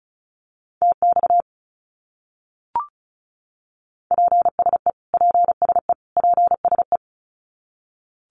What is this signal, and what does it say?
Morse code, heard as a keyed tone, about 34 wpm, 700 Hz. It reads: TX PSE PSE PSE
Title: Abbreviations Repeated 3x - 35wpm